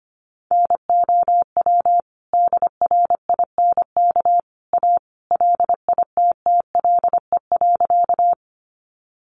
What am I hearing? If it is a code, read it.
Morse code, heard as a keyed tone, about 25 wpm, 700 Hz. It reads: NOW DRINK A LITTLE.
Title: Sentences from Top 1500 Words 25wpm